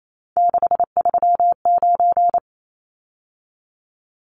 Morse code, heard as a keyed tone, about 28 wpm, 700 Hz. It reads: 639
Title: Numbers - 3-Digit 28wpm